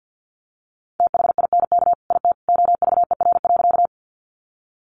Morse code, heard as a keyed tone, about 50 wpm, 700 Hz. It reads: THINK IT OVER.
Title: Sentences from Top 200 Words 50wpm